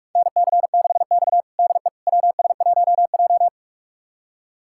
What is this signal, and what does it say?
Morse code, heard as a keyed tone, about 42 wpm, 700 Hz. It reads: NC6X DE WS1J
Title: USA Callsign DE Callsign - 4 Characters - 45wpm